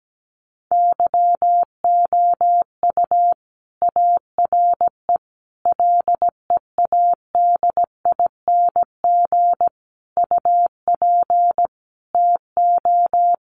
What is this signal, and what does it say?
Morse code, heard as a keyed tone, about 17 wpm, 700 Hz. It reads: YOU ARE LEADING UP TO
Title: Sentences from Top 1100 Words 17wpm